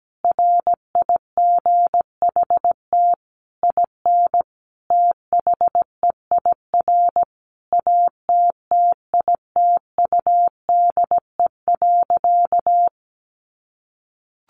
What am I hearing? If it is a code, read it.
Morse code, heard as a keyed tone, about 17 wpm, 700 Hz. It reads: RIGHT IN THEIR ATTITUDE.
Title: Sentences from Top 1300 Words 17wpm